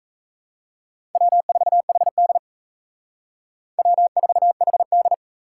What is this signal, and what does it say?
Morse code, heard as a keyed tone, about 35 wpm, 700 Hz. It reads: W4HD W4HD
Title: Speed Racing - USA 1x2 Call Signs 28wpm